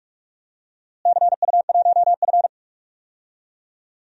Morse code, heard as a keyed tone, about 46 wpm, 700 Hz. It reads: CU1F
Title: DXCC-100 Call Signs - 4 Characters - Repeated 2x - 45wpm